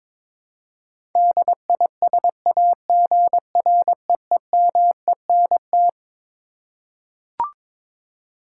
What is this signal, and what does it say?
Morse code, heard as a keyed tone, about 22 wpm, 700 Hz. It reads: DISAGREEMENT
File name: Twelve Letter Words - 22wpm